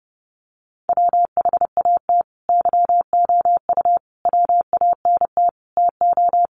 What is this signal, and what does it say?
Morse code, heard as a keyed tone, about 30 wpm, 700 Hz. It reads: WHAT YOU WANT TO
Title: Sets of 5 Words Encore 30wpm